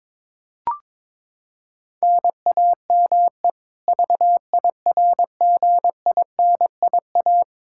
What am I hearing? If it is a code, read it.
Morse code, heard as a keyed tone, about 22 wpm, 700 Hz. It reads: NAME VIRGINIA
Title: Practice QSO - Exchange 1 - Beginning - 22wpm